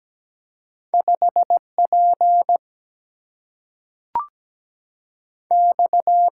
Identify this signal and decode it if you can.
Morse code, heard as a keyed tone, about 17 wpm, 700 Hz. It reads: 5 P X
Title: Single-Letter Single-Number 17wpm